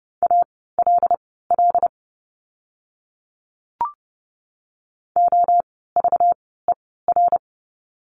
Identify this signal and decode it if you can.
Morse code, heard as a keyed tone, about 30 wpm, spaced as for 17 wpm, 700 Hz. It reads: ALL OVER
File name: course / 30:10wpm / 230 Lesson 37f - Letter x - Words 30wpm